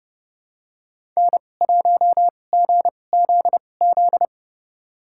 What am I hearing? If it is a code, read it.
Morse code, heard as a keyed tone, about 31 wpm, spaced as for 22 wpm, 700 Hz. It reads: N1GZZ
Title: USA Call Signs - 5 Characters - 2x Character Spacing - 30wpm